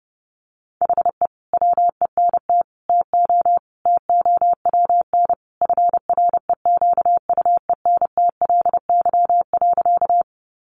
Morse code, heard as a keyed tone, about 30 wpm, 700 Hz. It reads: HE WENT TO TOWN FREQUENTLY.
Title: Sentences from Top 1300 Words 30wpm